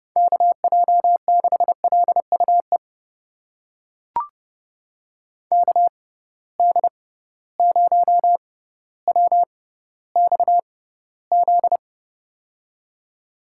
Morse code, heard as a keyed tone, about 32 wpm, spaced as for 16 wpm, 700 Hz. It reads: KJ6LUE KD0WXZ
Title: USA Call Signs - 6 Characters - 6x Character Spacing - 30wpm